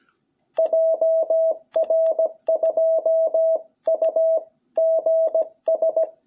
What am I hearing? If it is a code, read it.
Morse code, heard as a keyed tone, about 17 wpm, 660 Hz. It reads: JR2UGS
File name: jh8cbh / 2021cw